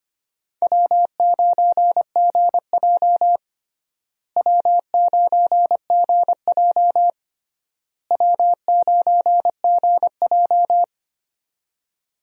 Morse code, heard as a keyed tone, about 25 wpm, 700 Hz. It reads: W9GJ W9GJ W9GJ
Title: USA Call Signs - 4 Characters - Repeated 3x - 25wpm